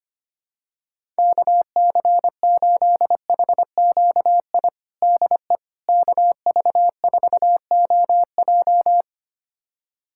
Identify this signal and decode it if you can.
Morse code, heard as a keyed tone, about 25 wpm, 700 Hz. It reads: KC8HQI DE KV4OJ
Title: USA Callsign DE Callsign 25wpm